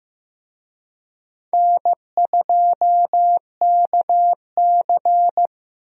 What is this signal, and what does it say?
Morse code, heard as a keyed tone, about 15 wpm, 700 Hz. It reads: N2KC
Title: USA Call Signs - 4 Characters - Repeated 3x - 15wpm